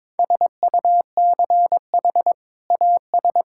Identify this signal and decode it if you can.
Morse code, heard as a keyed tone, about 21 wpm, 700 Hz. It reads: SUCH AS
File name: practice / 22wpm / Sentences with 2-Word Phrases - 22wpm